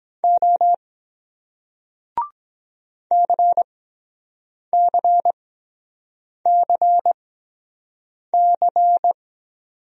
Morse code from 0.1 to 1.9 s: O

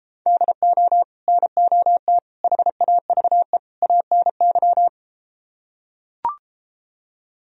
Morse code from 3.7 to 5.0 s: ANY

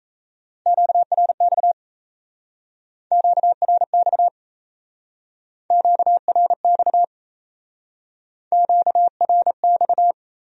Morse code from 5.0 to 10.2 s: QRX QRX

